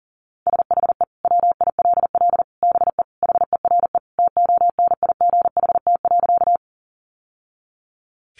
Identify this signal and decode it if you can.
Morse code, heard as a keyed tone, about 37 wpm, 700 Hz. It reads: SHE WILL BE HERE TONIGHT.
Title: Sentences from Top 600 Words 40wpm